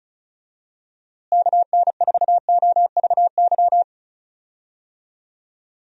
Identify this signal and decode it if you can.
Morse code, heard as a keyed tone, about 34 wpm, 700 Hz. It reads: KN4OVY
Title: USA Call Signs - 6 Characters - Repeated 3x - 35wpm